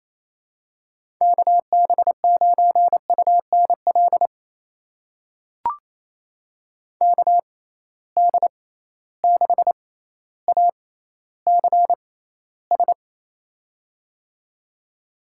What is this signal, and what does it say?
Morse code, heard as a keyed tone, about 28 wpm, spaced as for 14 wpm, 700 Hz. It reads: KB9UNL KD6ACS
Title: USA Call Signs - 6 Characters - 6x Character Spacing - 28wpm